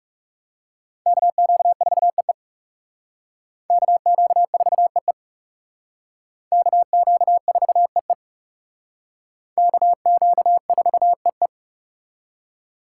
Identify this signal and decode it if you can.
Morse code, heard as a keyed tone, about 35 wpm, 700 Hz. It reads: KQ4EE KQ4EE KQ4EE KQ4EE
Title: Speed Racing - USA 2x2 Call Signs 30wpm